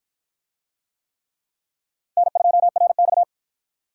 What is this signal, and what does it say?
Morse code, heard as a keyed tone, about 50 wpm, 700 Hz. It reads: N2RX